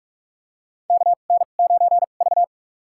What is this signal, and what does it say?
Morse code, heard as a keyed tone, about 46 wpm, spaced as for 33 wpm, 700 Hz. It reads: KN9V
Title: USA Call Signs - 4 Characters - 2x Character Spacing - 45wpm